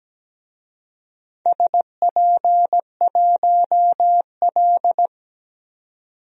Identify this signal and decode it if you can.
Morse code, heard as a keyed tone, about 17 wpm, 700 Hz. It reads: SP1L
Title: DXCC-100 Call Signs - 4 Characters - 17wpm